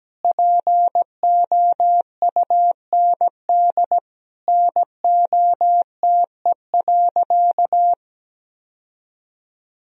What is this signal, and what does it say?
Morse code, heard as a keyed tone, about 17 wpm, 700 Hz. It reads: POUND NOTE.
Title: Sentences from Top 1000 Words - Review Part 2 - 17wpm